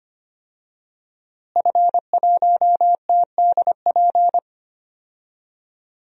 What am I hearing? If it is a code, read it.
Morse code, heard as a keyed tone, about 25 wpm, 700 Hz. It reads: F1TDP
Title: DXCC-100 Call Signs - 5 Characters - Repeated 2x - 25wpm